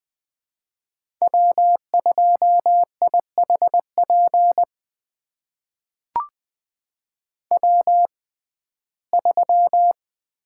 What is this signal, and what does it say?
Morse code, heard as a keyed tone, about 21 wpm, spaced as for 15 wpm, 700 Hz. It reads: W2IHP W3
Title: USA Call Signs - 5 Characters - 6x Character Spacing - 20wpm